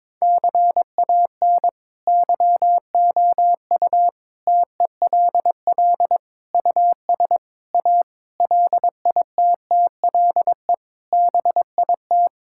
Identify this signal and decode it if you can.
Morse code, heard as a keyed tone, about 22 wpm, 700 Hz. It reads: CAN YOU TELL US A LITTLE BIT
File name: practice / 22wpm / Sets of 8 Words - 22wpm